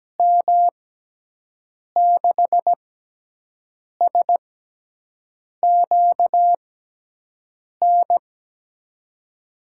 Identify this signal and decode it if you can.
Morse code, heard as a keyed tone, about 17 wpm, spaced as for 6 wpm, 700 Hz. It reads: M6SQN
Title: USA Call Signs - 6x Character Spacing - 17wpm